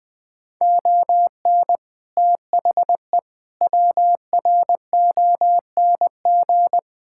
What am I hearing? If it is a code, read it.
Morse code, heard as a keyed tone, about 20 wpm, 700 Hz. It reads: ON THE WRONG
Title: Sets of 6 Words - 20wpm